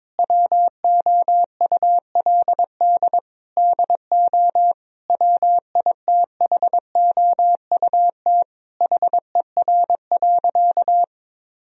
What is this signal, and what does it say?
Morse code, heard as a keyed tone, about 22 wpm, 700 Hz. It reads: WOULD DO WITHOUT HER.
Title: Sentences from Top 1000 Words - Review Part 3 - 22wpm